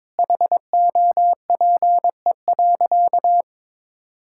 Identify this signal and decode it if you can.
Morse code, heard as a keyed tone, about 22 wpm, 700 Hz. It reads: HOPE.
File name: practice / 22wpm / Sentences with 3-Word Phrases - 22wpm